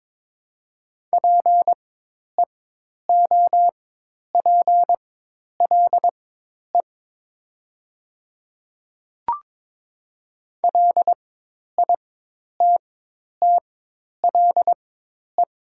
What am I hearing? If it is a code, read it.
Morse code, heard as a keyed tone, about 22 wpm, spaced as for 10 wpm, 700 Hz. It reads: PEOPLE LITTLE
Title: 4x Character Spacing - Top 200 Words - 22wpm